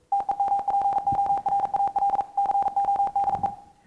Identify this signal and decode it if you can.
Morse code, heard as a keyed tone, about 42 wpm, 800 Hz. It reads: TEMPORAL CODE